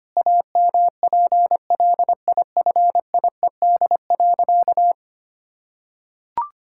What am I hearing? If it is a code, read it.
Morse code, heard as a keyed tone, about 25 wpm, 700 Hz. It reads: AMPLIFIED.
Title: Pangram Challenge v2 - 25wpm